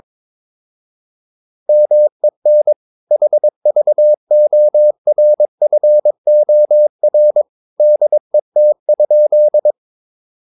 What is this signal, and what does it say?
Morse code, heard as a keyed tone, about 22 wpm, 600 Hz. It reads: MEN HVORFOR DET?
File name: Ferie QRS 22wpm